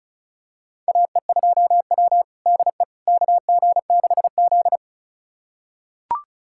Morse code, heard as a keyed tone, about 36 wpm, 700 Hz. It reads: AE2W DE KG6Z
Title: USA Callsign DE Callsign - 4 Characters - 35wpm